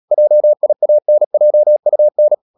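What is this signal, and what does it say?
Morse code, heard as a keyed tone, about 37 wpm, 600 Hz. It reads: JIANJUN
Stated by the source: Morse code practice at 35–39 wpm sending English text of Ken Follett's novel Never, from CW med Gnister.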